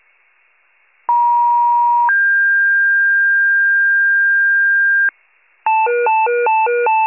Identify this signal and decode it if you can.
Two-tone page, heard as a one-second tone, then a three-second tone, about 950 then 1600 Hz, with noise. It is heard from 1.1 to 5.1 s.